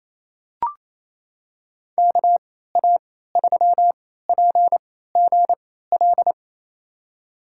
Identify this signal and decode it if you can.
Morse code, heard as a keyed tone, about 28 wpm, spaced as for 16 wpm, 700 Hz. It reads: KA3PGL
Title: USA Call Signs - 6 Characters - 3x Character Spacing - 28wpm